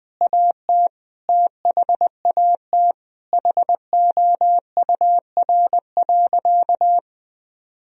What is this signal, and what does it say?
Morse code, heard as a keyed tone, about 20 wpm, 700 Hz. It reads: AT THAT HOUR.
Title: Sentences from Top 1000 Words - Review Part 3 - 20wpm